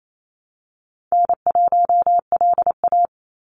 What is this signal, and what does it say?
Morse code, heard as a keyed tone, about 28 wpm, 700 Hz. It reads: N1LA
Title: USA Call Signs 28wpm